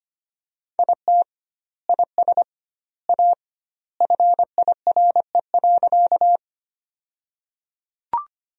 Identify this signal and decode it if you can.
Morse code, heard as a keyed tone, about 25 wpm, 700 Hz. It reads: IT IS A FIRE.
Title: 2x Word-Spacing - Sentences from Top 400 Words - 25wpm